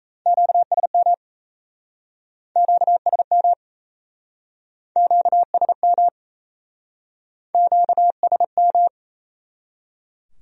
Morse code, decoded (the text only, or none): QSM QSM QSM QSM